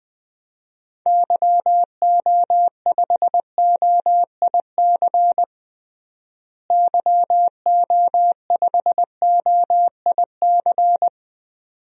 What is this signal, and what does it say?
Morse code, heard as a keyed tone, about 20 wpm, 700 Hz. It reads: YO5OIC YO5OIC
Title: DXCC-100 Call Signs - 6 Characters - Repeated 2x - 20wpm